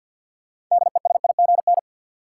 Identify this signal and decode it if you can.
Morse code, heard as a keyed tone, about 51 wpm, 700 Hz. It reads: DESIGN
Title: Sentences from Top 800 Words - New Word Drill - 50wpm